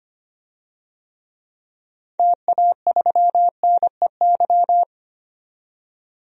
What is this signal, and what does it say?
Morse code, heard as a keyed tone, about 25 wpm, 700 Hz. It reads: TA3NEY
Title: DXCC-100 Call Signs - 6 Characters - 25wpm